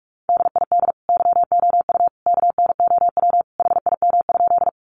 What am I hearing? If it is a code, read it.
Morse code, heard as a keyed tone, about 44 wpm, 700 Hz. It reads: DID YOU KNOW HIM?
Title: Sentences from Top 100 Words 45wpm